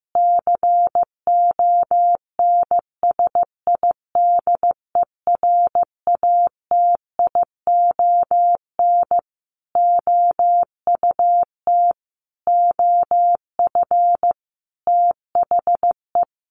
Morse code, heard as a keyed tone, about 15 wpm, 700 Hz. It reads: CONSIDERATION OUT OF THE
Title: Sentences from Top 1500 Words 15wpm